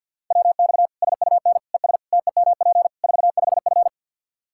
Morse code, heard as a keyed tone, about 51 wpm, 700 Hz. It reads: WX SUN ES TEMP 45F
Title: Practice QSO - Exchange 2 - Info - 50wpm